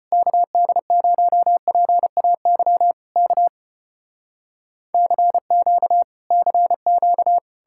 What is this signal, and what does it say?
Morse code, heard as a keyed tone, about 34 wpm, 700 Hz. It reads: KD0PAY K CQ CQ